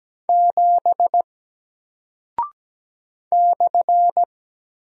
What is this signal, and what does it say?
Morse code, heard as a keyed tone, about 17 wpm, 700 Hz. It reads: 7 /
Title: Single Character (Letter:Number:Punctuation:Prosign) Alt 17wpm